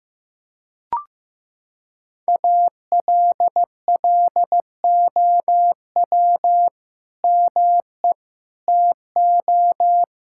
Morse code, heard as a keyed tone, about 15 wpm, 700 Hz. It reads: ALLOW ME TO
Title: Sentences from Top 1600 Words 15wpm